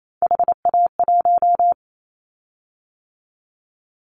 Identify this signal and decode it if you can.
Morse code, heard as a keyed tone, about 28 wpm, 700 Hz. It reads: HA1
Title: Callsign Prefixes for Top 50 DXCC - 28wpm